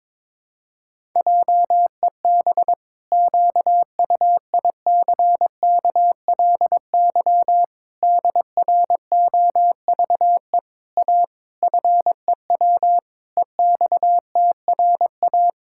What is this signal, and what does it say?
Morse code, heard as a keyed tone, about 22 wpm, 700 Hz. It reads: JEB QUICKLY DROVE A FEW EXTRA
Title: Pangram Challenge v2 - 22wpm